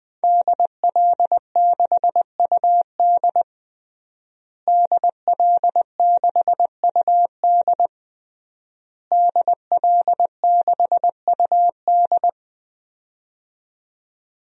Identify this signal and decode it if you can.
Morse code, heard as a keyed tone, about 20 wpm, 700 Hz. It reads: DL6UD DL6UD DL6UD
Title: DXCC-100 Call Signs - 5 Characters - Repeated 3x - 20wpm